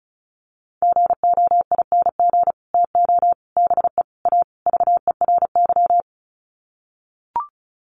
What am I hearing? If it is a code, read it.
Morse code, heard as a keyed tone, about 34 wpm, 700 Hz. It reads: GOING TO BE A VERY
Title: Sets of 5 Words Encore 35wpm